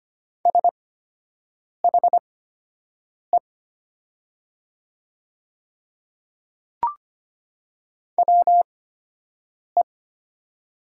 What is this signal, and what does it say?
Morse code, heard as a keyed tone, about 25 wpm, spaced as for 7 wpm, 690 Hz. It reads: SHE WE
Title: 8x Character Spacing - Top 200 Words - 25wpm